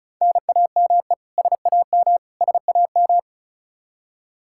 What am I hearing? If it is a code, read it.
Morse code, heard as a keyed tone, about 36 wpm, 700 Hz. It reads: NAME SAM SAM
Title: Practice QSO - Exchange 1 - Beginning - 35wpm